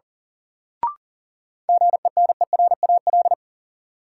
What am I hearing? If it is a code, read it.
Morse code, heard as a keyed tone, about 37 wpm, 700 Hz. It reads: GENERAL
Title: Sentences from Top 600 Words - New Word Drill - 40wpm